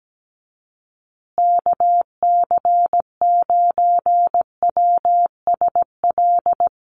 Morse code, heard as a keyed tone, about 17 wpm, 700 Hz. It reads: KC9WSL
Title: USA Call Signs Repeated 2x 17wpm